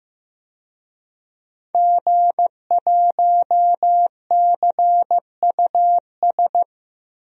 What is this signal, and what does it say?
Morse code, heard as a keyed tone, about 15 wpm, 700 Hz. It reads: G1CUS
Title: DXCC-100 Call Signs - 5 Characters - 15wpm